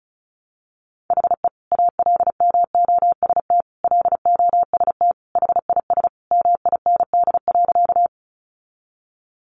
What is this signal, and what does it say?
Morse code, heard as a keyed tone, about 34 wpm, 700 Hz. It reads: HE ALMOST LOST HIS MIND.
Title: Sentences from Top 1400 Words 35wpm